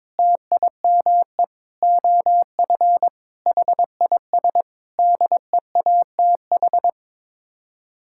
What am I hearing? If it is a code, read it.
Morse code, heard as a keyed tone, about 22 wpm, 700 Hz. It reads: TIME OF HIS DEATH
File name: practice / 22wpm / Sets of 6 Words - 22wpm